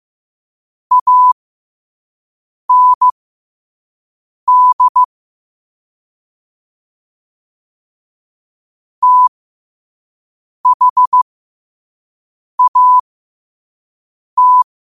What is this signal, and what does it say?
Morse code, heard as a keyed tone, about 15 wpm, spaced as for 5 wpm, 1000 Hz. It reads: AND THAT
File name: qotdfiles / 5wpm